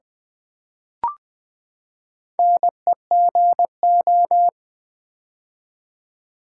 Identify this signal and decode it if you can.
Morse code, heard as a keyed tone, about 20 wpm, 700 Hz. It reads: NEGO
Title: Build A Word - 4 Letters At A Time - 11-Letter Words - 20wpm